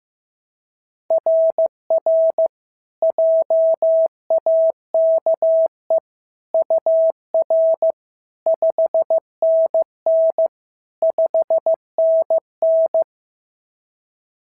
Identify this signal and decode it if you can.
Morse code, heard as a keyed tone, about 14 wpm, 650 Hz. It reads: RR JAKE UR 5NN 5NN
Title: SOTA QSO Practice 15wpm